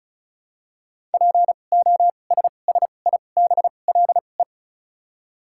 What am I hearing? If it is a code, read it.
Morse code, heard as a keyed tone, about 34 wpm, spaced as for 25 wpm, 700 Hz. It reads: POSSIBLE
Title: 2x Character Spacing - Top 500 Words - 35wpm